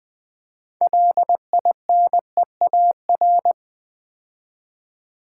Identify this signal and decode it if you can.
Morse code, heard as a keyed tone, about 20 wpm, 700 Hz. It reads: LINEAR
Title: 6 Letter Words - First 3 Letters Given - 20wpm